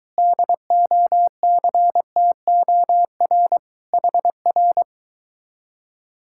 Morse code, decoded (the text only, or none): DOCTOR HR